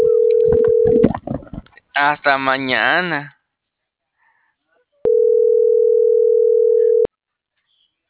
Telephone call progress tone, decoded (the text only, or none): ringback tone